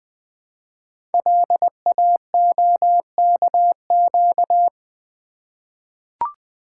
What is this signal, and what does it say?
Morse code, heard as a keyed tone, about 20 wpm, 700 Hz. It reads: LAOKQ
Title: Code Groups - 5 Letters - 20wpm